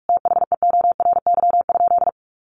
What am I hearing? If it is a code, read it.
Morse code, heard as a keyed tone, about 44 wpm, 700 Hz. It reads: THEORY?
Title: Sentences from Top 1000 Words 45wpm